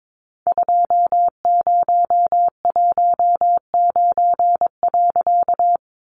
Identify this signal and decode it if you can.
Morse code, heard as a keyed tone, about 22 wpm, 700 Hz. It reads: 2019.
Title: A Sound Of Thunder 22wpm